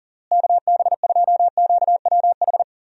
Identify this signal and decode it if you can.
Morse code, heard as a keyed tone, about 40 wpm, 700 Hz. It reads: KB2QWH
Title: USA Call Signs - 6 Characters - Repeated 3x - 40wpm